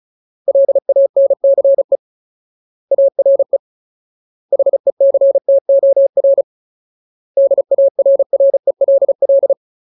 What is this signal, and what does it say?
Morse code, heard as a keyed tone, about 35 wpm, 550 Hz. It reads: LANCE ARE HECTOR DARRELL
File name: Ultimate Word List / 35WPM Ultimate